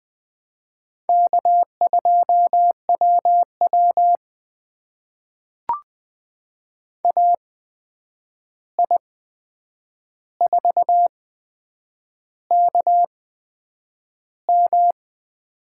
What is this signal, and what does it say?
Morse code, heard as a keyed tone, about 20 wpm, spaced as for 8 wpm, 700 Hz. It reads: K2WW AI4KM
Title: USA Call Signs - 8x Character Spacing - 20wpm